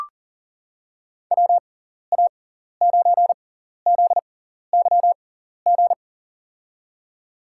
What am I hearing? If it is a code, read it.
Morse code, heard as a keyed tone, about 40 wpm, spaced as for 14 wpm, 700 Hz. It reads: WA9ZYG
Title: USA Call Signs - 6 Characters - 6x Character Spacing - 40wpm